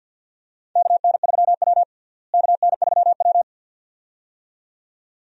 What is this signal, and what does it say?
Morse code, heard as a keyed tone, about 49 wpm, 700 Hz. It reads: KN3W KN3W
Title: USA Call Signs - 4 Characters - Repeated 2x - 50wpm